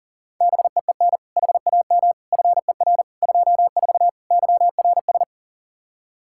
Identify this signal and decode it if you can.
Morse code, heard as a keyed tone, about 40 wpm, 700 Hz. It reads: BEEN HAM FER 24 YRS
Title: Practice QSO - Exchange 3 - More Info - 40wpm